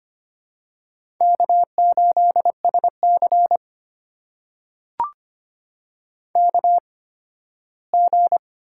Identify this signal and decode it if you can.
Morse code, heard as a keyed tone, about 25 wpm, spaced as for 15 wpm, 700 Hz. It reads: K8SC KG
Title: USA Call Signs - 8x Character Spacing - 25wpm